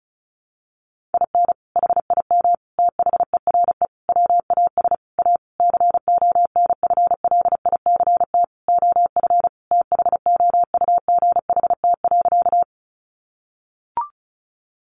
Morse code, heard as a keyed tone, about 34 wpm, 700 Hz. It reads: IN HIM THERE WAS A CONFLICT OF THOUGHT.
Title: Sentences from Top 1300 Words 35wpm